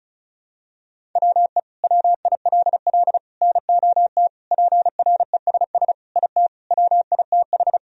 Morse code, heard as a keyed tone, about 36 wpm, 700 Hz. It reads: WE WILL NOT PRESS IT WITH
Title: Sentences from Top 1000 Words - Review Part 1 - 35wpm